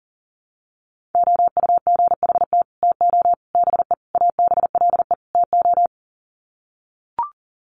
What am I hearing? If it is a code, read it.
Morse code, heard as a keyed tone, about 40 wpm, 700 Hz. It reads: OUGHT TO BE ABLE TO
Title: Sets of 5 Words Encore2 40wpm